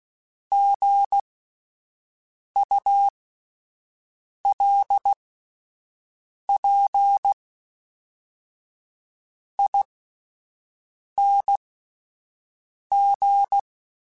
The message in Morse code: GULPING

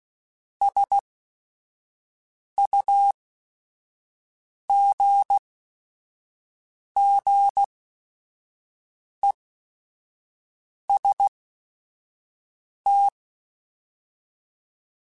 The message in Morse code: SUGGEST

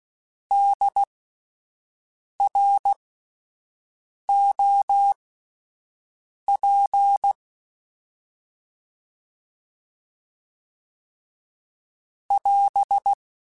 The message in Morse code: DROP <AS>

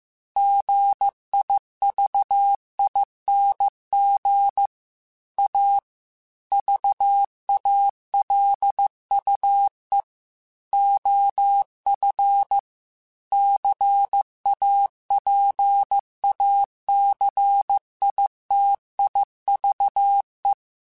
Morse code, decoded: GIVING A VALUE OF CAPACITIVE

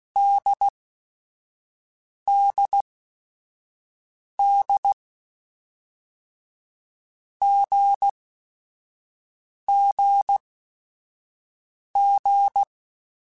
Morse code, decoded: DDD GGG

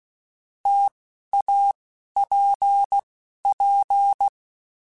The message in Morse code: T A P P

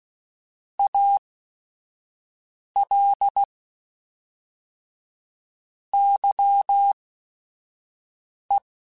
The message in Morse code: AL YE

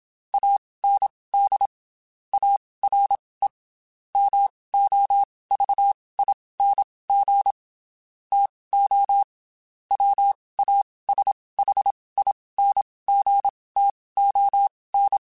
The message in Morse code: AND ARE MOVING TO WASHINGTON

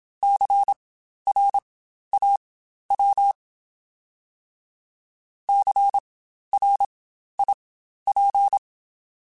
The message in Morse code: CRAW CRIP

